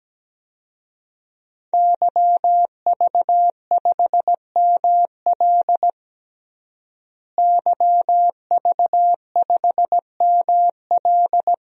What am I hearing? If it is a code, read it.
Morse code, heard as a keyed tone, about 17 wpm, 700 Hz. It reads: YV5ML YV5ML
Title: DXCC-100 Call Signs - 5 Characters - Repeated 3x - 17wpm